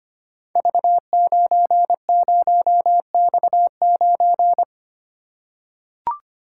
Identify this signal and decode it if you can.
Morse code, heard as a keyed tone, about 25 wpm, 700 Hz. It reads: V90X9
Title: Code Groups - 5 Letter:Numbers - 25wpm